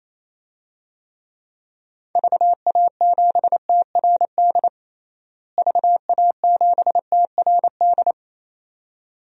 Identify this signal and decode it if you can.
Morse code, heard as a keyed tone, about 28 wpm, 700 Hz. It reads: VA7TRD VA7TRD
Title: DXCC-100 Call Signs - 6 Characters - Repeated 2x - 28wpm